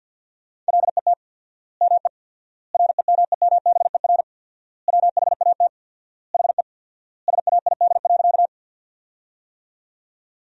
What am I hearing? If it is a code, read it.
Morse code, heard as a keyed tone, about 49 wpm, 700 Hz. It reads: LET ME REMEMBER WHAT HE SAID.
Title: 4x Word-Spacing - Sentences from Top 400 Words - 50wpm